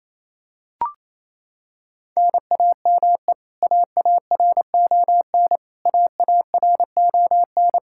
Morse code, heard as a keyed tone, about 27 wpm, 700 Hz. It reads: NAME AARON AARON
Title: Practice QSO - Exchange 1 - Beginning - 28wpm